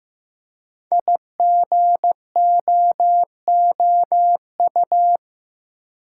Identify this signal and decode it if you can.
Morse code, heard as a keyed tone, about 15 wpm, 700 Hz. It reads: IGOOU
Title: Code Groups - 5 Letters - 15wpm